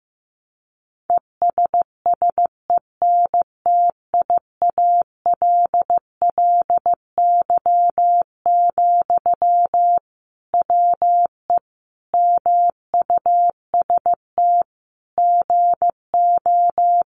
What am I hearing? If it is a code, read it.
Morse code, heard as a keyed tone, about 15 wpm, 700 Hz. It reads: ESSENTIALLY, WE MUST GO
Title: Sentences from Top 1300 Words 15wpm